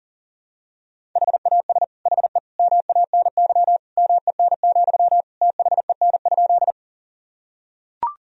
Morse code, heard as a keyed tone, about 37 wpm, 700 Hz. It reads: HAS HE MANY MEN, THEN?
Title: Sentences from Top 1000 Words - Review Part 1 - 40wpm